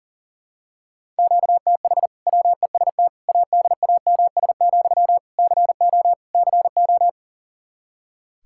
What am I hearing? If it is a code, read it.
Morse code, heard as a keyed tone, about 37 wpm, 700 Hz. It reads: QTH WEST ADAMS, CO CO
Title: Practice QSO - Exchange 1 - Beginning - 40wpm